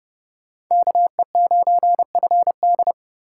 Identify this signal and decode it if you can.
Morse code, heard as a keyed tone, about 28 wpm, 700 Hz. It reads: KE9FD